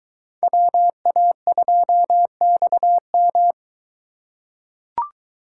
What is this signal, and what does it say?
Morse code, heard as a keyed tone, about 21 wpm, 700 Hz. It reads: WA2XM